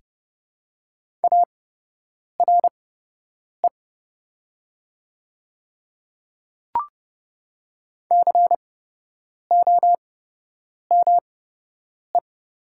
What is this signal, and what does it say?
Morse code, heard as a keyed tone, about 30 wpm, spaced as for 8 wpm, 690 Hz. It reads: ARE COME
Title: 8x Character Spacing - Top 200 Words - 30wpm